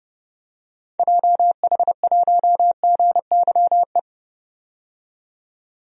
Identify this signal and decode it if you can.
Morse code, heard as a keyed tone, about 30 wpm, 700 Hz. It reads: JH1GYE
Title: DXCC-100 Call Signs - 6 Characters - 30wpm